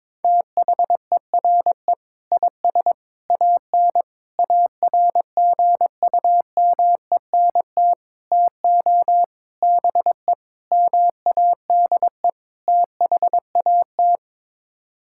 Morse code, heard as a keyed tone, about 22 wpm, 700 Hz. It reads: THERE IS AN ARGUMENT TO BE MADE THAT
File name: practice / 22wpm / Sets of 8 Words - 22wpm